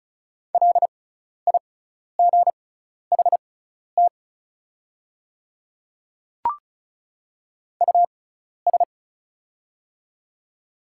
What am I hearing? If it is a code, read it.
Morse code, heard as a keyed tone, about 34 wpm, spaced as for 12 wpm, 690 Hz. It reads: LIGHT US
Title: 6x Character Spacing - Top 300 Words - 35wpm